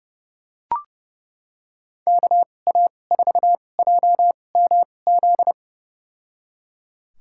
Morse code, heard as a keyed tone, about 32 wpm, spaced as for 22 wpm, 700 Hz. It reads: KA4JMZ
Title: USA Call Signs - 6 Characters - 2x Character Spacing - 30wpm